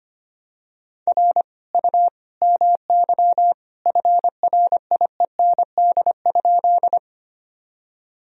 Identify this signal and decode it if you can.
Morse code, heard as a keyed tone, about 25 wpm, 700 Hz. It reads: R U MY FRIEND?
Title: Sentences from Top 300 Words and Abbreviations 25wpm